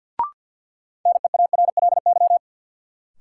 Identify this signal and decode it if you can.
Morse code, heard as a keyed tone, about 52 wpm, 700 Hz. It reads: NEARLY
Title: Sentences from Top 1300 Words - New Word Drill Alt - 50wpm